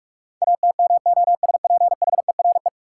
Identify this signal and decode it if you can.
Morse code, heard as a keyed tone, about 44 wpm, 700 Hz. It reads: ATMOSPHERE
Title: Sentences from Top 1600 Words - New Word Drill - 45wpm